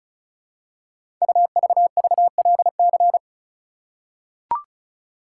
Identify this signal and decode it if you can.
Morse code, heard as a keyed tone, about 36 wpm, 690 Hz. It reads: UVVLC
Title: Code Groups - 5 Letters - 35wpm